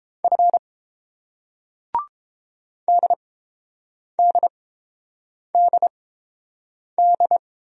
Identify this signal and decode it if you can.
Morse code, heard as a keyed tone, about 30 wpm, spaced as for 7 wpm, 690 Hz. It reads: F DDDD